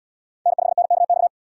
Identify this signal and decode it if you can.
Morse code, heard as a keyed tone, about 77 wpm, 700 Hz. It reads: N5NKX